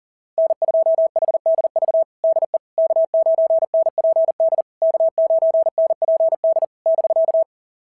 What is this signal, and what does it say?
Morse code, heard as a keyed tone, about 40 wpm, 690 Hz. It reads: N2HDV DE K9NPD K9NPD <BK>
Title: SOTA QSO Practice 40wpm